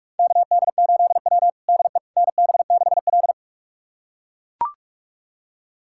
Morse code, heard as a keyed tone, about 44 wpm, 700 Hz. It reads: KD8W DE NB6L
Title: USA Callsign DE Callsign - 4 Characters - 45wpm